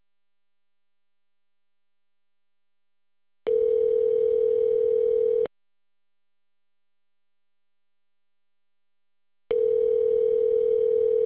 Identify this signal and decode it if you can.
Ringback tone (North America), 440 + 480 Hz.